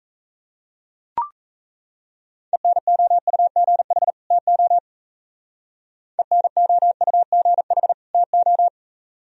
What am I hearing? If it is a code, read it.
Morse code, heard as a keyed tone, about 33 wpm, 700 Hz. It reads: ENOUGH TO ENOUGH TO
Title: Speed-Racing - Sets of 2 Words 28wpm